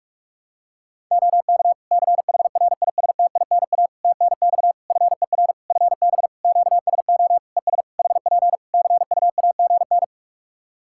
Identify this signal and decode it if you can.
Morse code, heard as a keyed tone, about 46 wpm, 700 Hz. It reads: OK CHRISTINA TNX FER FB QSO ES HP CUAGN
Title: Practice QSO - Exchange 4 - Ending - 45wpm